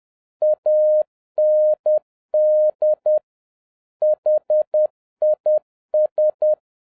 Morse code, heard as a keyed tone, about 10 wpm, 610 Hz. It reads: AND HIS